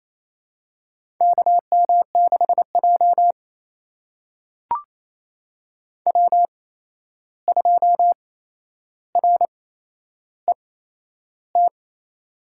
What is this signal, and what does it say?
Morse code, heard as a keyed tone, about 28 wpm, spaced as for 11 wpm, 700 Hz. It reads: KM6J W2RET
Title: USA Call Signs - 8x Character Spacing - 28wpm